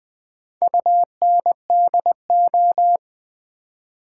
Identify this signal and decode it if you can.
Morse code, heard as a keyed tone, about 20 wpm, 700 Hz. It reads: UNDO
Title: Build A Word - 4 Letters At A Time - 11-Letter Words - 20wpm